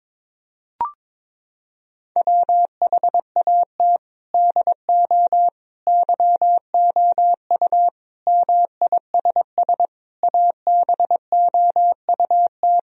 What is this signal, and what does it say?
Morse code, heard as a keyed tone, about 22 wpm, 700 Hz. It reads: WHAT DO YOU MISS ABOUT
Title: Sentences from Top 1000 Words - Review Part 3 - 22wpm